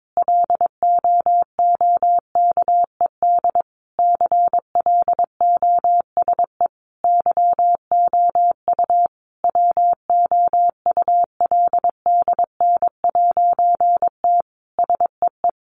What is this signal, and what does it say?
Morse code, heard as a keyed tone, about 22 wpm, 700 Hz. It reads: LOOKED CLOSE YOU WOULDN'T SEE
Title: A Sound Of Thunder 22wpm